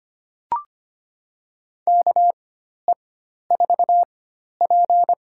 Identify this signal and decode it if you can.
Morse code, heard as a keyed tone, about 25 wpm, spaced as for 12 wpm, 700 Hz. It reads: KE4P
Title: USA Call Signs - 4x Character Spacing - 25wpm